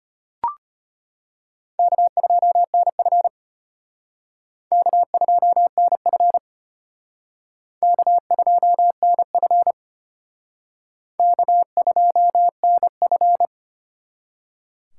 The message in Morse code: K2NF K2NF K2NF K2NF

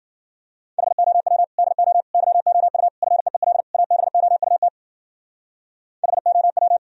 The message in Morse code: HOW DO YOU FEEL ABOUT HOW

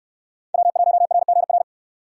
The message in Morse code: W1RGR

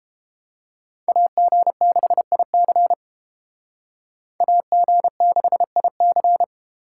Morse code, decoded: AG6IC AG6IC

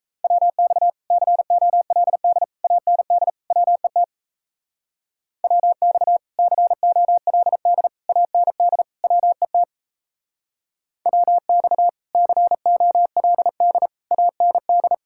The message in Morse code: WX COLD AND WET WX COLD AND WET WX COLD AND

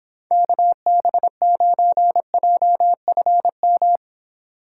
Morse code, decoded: KB9JFM

